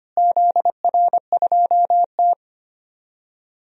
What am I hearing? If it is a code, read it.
Morse code, heard as a keyed tone, about 25 wpm, 700 Hz. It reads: ZR2T
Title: DXCC-100 Call Signs - 4 Characters - Repeated 3x - 25wpm